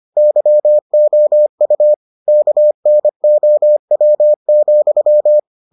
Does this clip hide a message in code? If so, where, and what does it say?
Yes: Morse code from 0.2 to 5.4 s, heard as a keyed tone, about 25 wpm, 600 Hz. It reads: YOU KNOW,